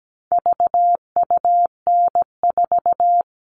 Morse code, heard as a keyed tone, about 17 wpm, 700 Hz. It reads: VUN4
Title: All Cut Numbers - 4-Digit 17wpm